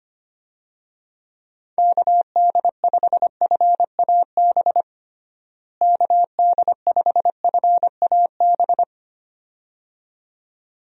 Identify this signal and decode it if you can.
Morse code, heard as a keyed tone, about 25 wpm, 700 Hz. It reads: KD5FAB KD5FAB
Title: USA Call Signs - 6 Characters - Repeated 2x - 25wpm